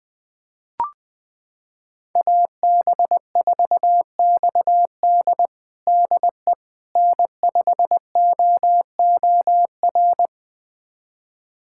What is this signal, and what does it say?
Morse code, heard as a keyed tone, about 20 wpm, 700 Hz. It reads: AB4XD DE N5OOR
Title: USA Callsign DE Callsign - 5 Characters - 20wpm